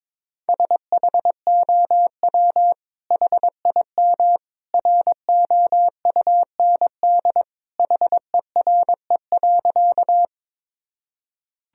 Morse code, heard as a keyed tone, about 22 wpm, 700 Hz. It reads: SHOW HIM ROUND HERE.
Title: Sentences from Top 1000 Words - Review Part 1 - 22wpm